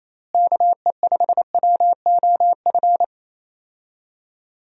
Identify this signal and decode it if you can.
Morse code, heard as a keyed tone, about 28 wpm, 700 Hz. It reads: KE5WOF
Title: USA Call Signs - 6 Characters - Repeated 3x - 28wpm